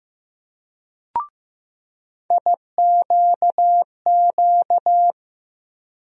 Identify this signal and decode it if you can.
Morse code, heard as a keyed tone, about 15 wpm, 700 Hz. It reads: IQQ
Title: USA Call Sign Postfixes - 3 Letters 15wpm